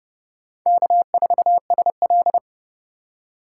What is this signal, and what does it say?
Morse code, heard as a keyed tone, about 29 wpm, 700 Hz. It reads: K4SL